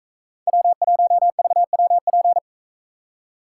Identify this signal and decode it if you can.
Morse code, heard as a keyed tone, about 44 wpm, 700 Hz. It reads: W1VWP